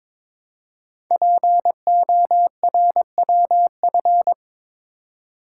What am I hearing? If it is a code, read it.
Morse code, heard as a keyed tone, about 22 wpm, 700 Hz. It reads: PORWF
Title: Code Groups - 5 Letters - 22wpm